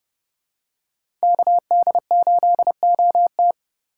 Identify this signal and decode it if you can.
Morse code, heard as a keyed tone, about 30 wpm, 700 Hz. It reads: KD8OT
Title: USA Call Signs - 5 Characters - 3x Character Spacing - 30wpm